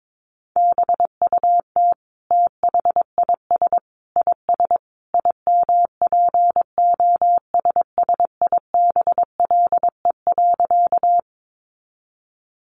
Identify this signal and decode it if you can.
Morse code, heard as a keyed tone, about 22 wpm, 700 Hz. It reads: BUT THIS IS IMPOSSIBLE.
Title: Sentences from Top 1100 Words 22wpm